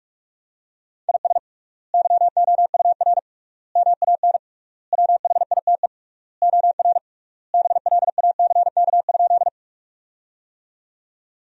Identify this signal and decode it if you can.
Morse code, heard as a keyed tone, about 46 wpm, 700 Hz. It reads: IS YOUR MAN WHITE OR BLACK?
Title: 3x Word-Spacing - Sentences from Top 400 Words - 45wpm